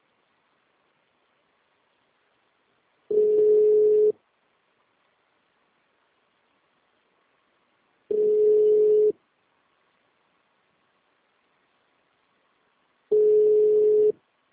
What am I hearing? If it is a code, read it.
Ringback tone (North America), 440 + 480 Hz.